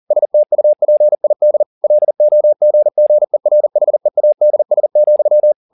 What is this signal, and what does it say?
Morse code, heard as a keyed tone, about 40 wpm, 600 Hz. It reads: STUPID LOGGERHEADS,